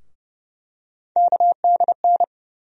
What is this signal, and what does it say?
Morse code, heard as a keyed tone, about 31 wpm, 700 Hz. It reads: KDN
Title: USA Call Sign Postfixes - 3 Letters 30wpm